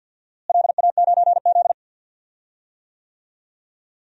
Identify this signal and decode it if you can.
Morse code, heard as a keyed tone, about 52 wpm, 700 Hz. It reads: LA9Z
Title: DXCC-100 Call Signs - 4 Characters - Repeated 2x - 50wpm